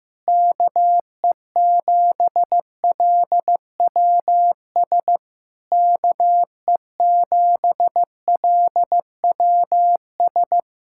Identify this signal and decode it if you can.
Morse code, heard as a keyed tone, about 15 wpm, 700 Hz. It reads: KE7LWS KE7LWS
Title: Speed-Racing Common QSO Phrases Encore - 15wpm